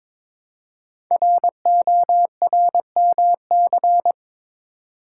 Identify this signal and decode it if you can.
Morse code, heard as a keyed tone, about 22 wpm, 700 Hz. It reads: RORMC
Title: Code Groups - 5 Letters - 22wpm